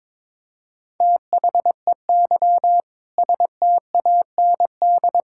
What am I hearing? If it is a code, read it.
Morse code, heard as a keyed tone, about 22 wpm, 700 Hz. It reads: THEY STAND
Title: Sentences from Top 1000 Words - Review Part 5 - 22wpm